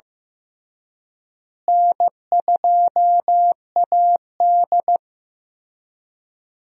Morse code, heard as a keyed tone, about 15 wpm, 700 Hz. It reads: N2AD
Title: USA Call Signs - 4 Characters - Repeated 3x - 15wpm